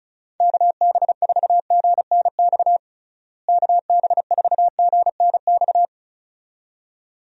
Morse code, heard as a keyed tone, about 36 wpm, 700 Hz. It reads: KB4GNX KB4GNX
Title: USA Call Signs - 6 Characters - Repeated 2x - 35wpm